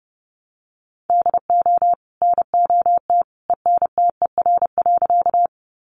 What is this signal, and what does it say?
Morse code, heard as a keyed tone, about 28 wpm, 700 Hz. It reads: DO NOT ENTER.
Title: Sentences from Top 1300 Words 30wpm